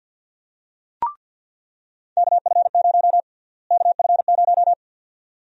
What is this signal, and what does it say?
Morse code, heard as a keyed tone, about 48 wpm, 700 Hz. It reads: KF0 KF0
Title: USA Call Sign Prefixes - Repeated 2x - 50wpm